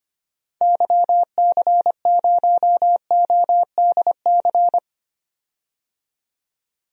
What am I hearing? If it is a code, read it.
Morse code, heard as a keyed tone, about 25 wpm, 700 Hz. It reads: YC0ODC
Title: DXCC-100 Call Signs - 6 Characters - 25wpm